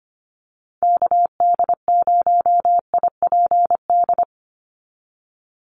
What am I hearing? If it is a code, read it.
Morse code, heard as a keyed tone, about 25 wpm, 700 Hz. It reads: KD0IPD
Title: USA Call Signs 25wpm